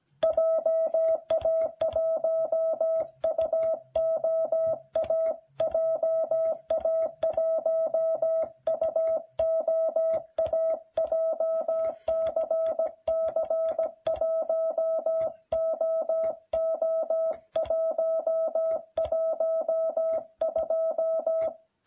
Morse code, heard as a keyed tone, about 17 wpm, 660 Hz. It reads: JA1UOA JA1UOA JCC1OO112